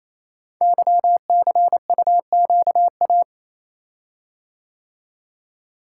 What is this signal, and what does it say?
Morse code, heard as a keyed tone, about 27 wpm, 690 Hz. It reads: YCUQA
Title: Code Groups - 5 Letters - 28wpm